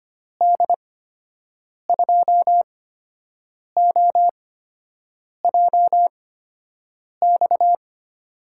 Morse code, heard as a keyed tone, about 25 wpm, spaced as for 7 wpm, 700 Hz. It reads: D2OJX
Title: USA Call Signs - 6 Characters - 8x Character Spacing - 25wpm